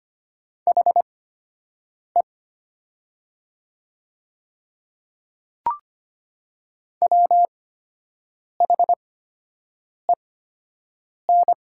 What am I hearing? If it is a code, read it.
Morse code, heard as a keyed tone, about 25 wpm, spaced as for 7 wpm, 700 Hz. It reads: HE WHEN
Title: 8x Character Spacing - Top 100 Words - 25wpm